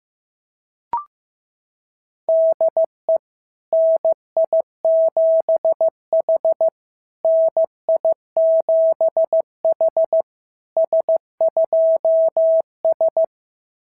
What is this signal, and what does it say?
Morse code, heard as a keyed tone, about 15 wpm, 650 Hz. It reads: DE NI7H NI7H S2S
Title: SOTA QSO Practice 15wpm